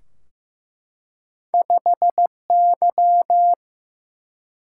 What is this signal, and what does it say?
Morse code, heard as a keyed tone, about 15 wpm, 700 Hz. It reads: 5Y
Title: Single-Letter Single-Number 15wpm